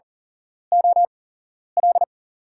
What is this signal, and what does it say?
Morse code, heard as a keyed tone, about 40 wpm, spaced as for 11 wpm, 700 Hz. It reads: OL
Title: USA Call Signs - 5 Characters - 8x Character Spacing - 40wpm